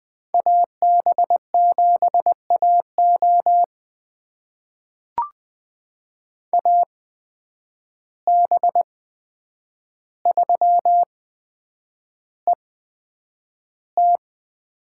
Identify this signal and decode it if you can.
Morse code, heard as a keyed tone, about 21 wpm, spaced as for 9 wpm, 700 Hz. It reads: AB7AO AB3ET
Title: USA Call Signs - 5 Characters - 8x Character Spacing - 20wpm